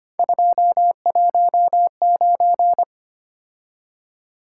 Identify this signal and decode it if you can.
Morse code, heard as a keyed tone, about 25 wpm, 700 Hz. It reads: 219
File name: Three Letter-Number Combinations - 25wpm